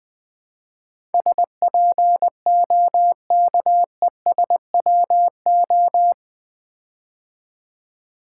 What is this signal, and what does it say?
Morse code, heard as a keyed tone, about 20 wpm, 700 Hz. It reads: SPOKESWO
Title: Build A Word - 4 Letters At A Time - 11-Letter Words - 20wpm